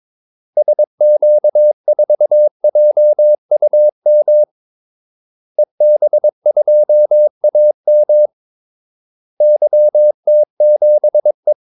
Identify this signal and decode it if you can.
Morse code, heard as a keyed tone, about 22 wpm, 600 Hz. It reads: SQ4JUM EB2AM YT7E
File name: Callsigns-22wpm